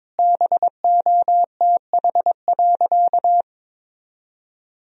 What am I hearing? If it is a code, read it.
Morse code, heard as a keyed tone, about 22 wpm, 700 Hz. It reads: BOTH.
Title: Sentences from Top 300 Words and Abbreviations 22wpm